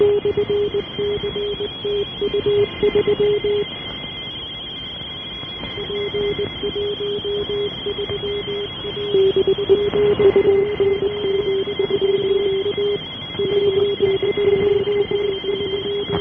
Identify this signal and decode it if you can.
Two-tone teleprinter signal, about 170 Hz shift.
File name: unm2 12703 1039